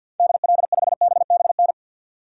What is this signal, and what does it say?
Morse code, heard as a keyed tone, about 52 wpm, 700 Hz. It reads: DL5BBN
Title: DXCC-100 Call Signs - 6 Characters - 50wpm